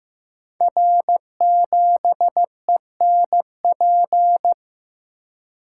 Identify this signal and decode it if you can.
Morse code, heard as a keyed tone, about 15 wpm, 700 Hz. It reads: R7ENP
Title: Code Groups - 5 Letter:Numbers - 15wpm